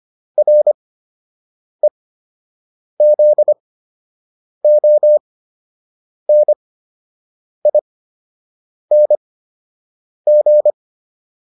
Morse code, heard as a keyed tone, about 25 wpm, spaced as for 7 wpm, 600 Hz. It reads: REZONING
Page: fundamental-practice-files